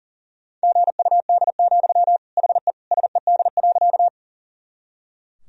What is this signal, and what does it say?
Morse code, heard as a keyed tone, about 40 wpm, 700 Hz. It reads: GUD, HE SED.
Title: Sentences from Top 200 Words and Abbreviations 40wpm